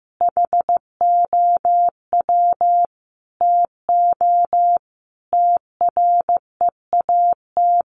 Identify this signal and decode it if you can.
Morse code, heard as a keyed tone, about 15 wpm, 700 Hz. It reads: HOW TO TREAT
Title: Sentences from Top 1300 Words 15wpm